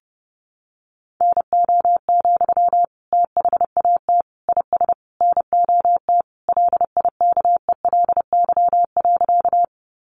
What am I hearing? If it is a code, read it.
Morse code, heard as a keyed tone, about 30 wpm, 700 Hz. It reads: NO, THAT IS NOT LIKELY.
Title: Sentences from Top 600 Words 30wpm